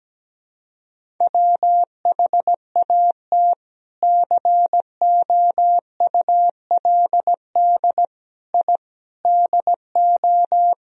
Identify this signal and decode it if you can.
Morse code, heard as a keyed tone, about 17 wpm, 700 Hz. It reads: WHAT COULD I DO
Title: Sentences from Top 1000 Words - Review Part 1 - 17wpm